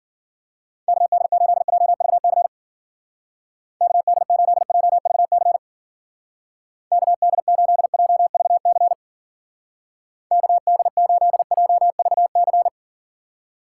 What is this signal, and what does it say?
Morse code, heard as a keyed tone, about 48 wpm, 700 Hz. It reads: KD8JVC KD8JVC KD8JVC KD8JVC
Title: Speed Racing - USA Call Signs 40wpm